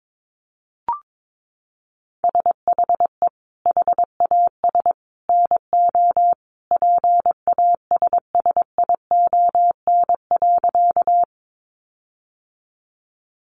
Morse code, heard as a keyed tone, about 22 wpm, 700 Hz. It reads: SHE HAS NO PASSION.
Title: Sentences from Top 1300 Words 22wpm